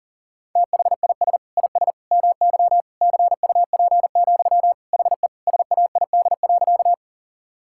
Morse code, heard as a keyed tone, about 40 wpm, 700 Hz. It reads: THIS IS MY CUP, HE SAID.
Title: Sentences from Top 1600 Words 40wpm